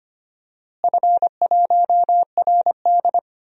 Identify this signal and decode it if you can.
Morse code, heard as a keyed tone, about 25 wpm, 700 Hz. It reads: F1RD
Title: DXCC-100 Call Signs - 4 Characters - Repeated 3x - 25wpm